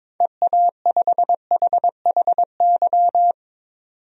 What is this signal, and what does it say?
Morse code, heard as a keyed tone, about 22 wpm, 700 Hz. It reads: EA5HHY
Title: DXCC-100 Call Signs - 6 Characters - 22wpm